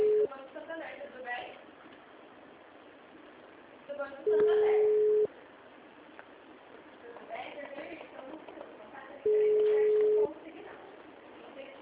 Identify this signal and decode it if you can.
Ringback tone (Europe, 425 Hz), with speech.